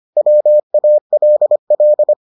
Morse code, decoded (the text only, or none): WALL